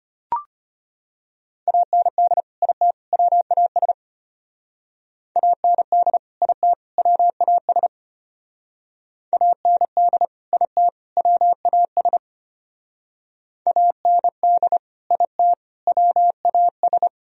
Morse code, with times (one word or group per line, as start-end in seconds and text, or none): 1.7-2.4 AND
2.6-2.9 IT
3.1-3.9 WAS
5.4-6.2 AND
6.4-6.7 IT
7.0-7.9 WAS
9.3-10.2 AND
10.5-10.9 IT
11.2-12.2 WAS
13.7-14.8 AND
15.1-15.5 IT
15.9-17.1 WAS